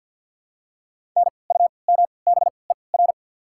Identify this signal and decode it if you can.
Morse code, heard as a keyed tone, about 53 wpm, spaced as for 29 wpm, 700 Hz. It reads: NUMBER
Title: 3x Character Spacing - Top 500 Words - 50wpm